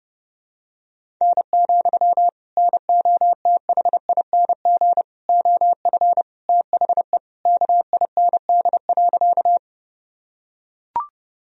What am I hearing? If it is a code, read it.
Morse code, heard as a keyed tone, about 30 wpm, 700 Hz. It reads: N, NOTHING OF THE KIND.
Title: Sentences from Top 500 Words and Abbreviations 30wpm